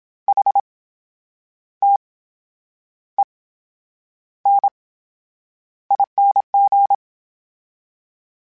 Morse code, heard as a keyed tone, about 27 wpm, spaced as for 9 wpm, 800 Hz. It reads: HTENING
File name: Special INGs / Special INGs-mp3s / VerySlowToFaster Ing @25